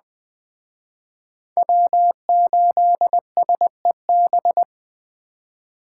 Morse code, heard as a keyed tone, about 21 wpm, 700 Hz. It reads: W8SEB